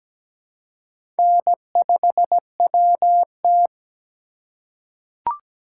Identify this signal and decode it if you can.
Morse code, heard as a keyed tone, about 17 wpm, 700 Hz. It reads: N5WT
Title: USA Call Signs - 4 Characters - 2x Character Spacing - 17wpm